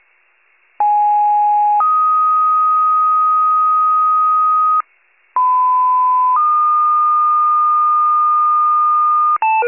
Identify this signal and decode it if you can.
Two-tone page, heard as a one-second tone, then a three-second tone, about 830 then 1250 Hz, about 1010 then 1220 Hz. It is heard from 0.8 to 9.4 s.
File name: wcalert